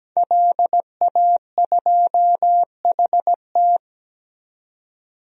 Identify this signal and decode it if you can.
Morse code, heard as a keyed tone, about 17 wpm, 700 Hz. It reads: LA2HT
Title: DXCC-100 Call Signs - 5 Characters - 17wpm